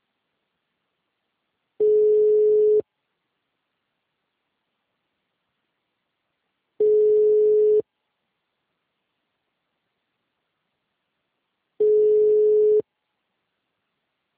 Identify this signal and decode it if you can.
Ringback tone (Europe, 425 Hz).
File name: out- -2010- - - .36430